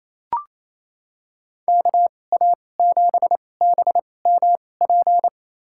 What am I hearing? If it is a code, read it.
Morse code, heard as a keyed tone, about 28 wpm, spaced as for 20 wpm, 690 Hz. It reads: KA7BMP
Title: USA Call Signs - 2x Character Spacing - 28wpm